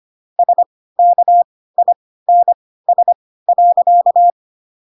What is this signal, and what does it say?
Morse code, heard as a keyed tone, about 25 wpm, spaced as for 16 wpm, 700 Hz. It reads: SKINS.
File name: Pinocchio / Pinocchio ch7 10 25 ik1hge